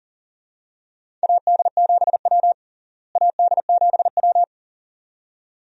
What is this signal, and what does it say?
Morse code, heard as a keyed tone, about 40 wpm, 700 Hz. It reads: AD7W AD7W
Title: USA Call Signs - 4 Characters - Repeated 2x - 40wpm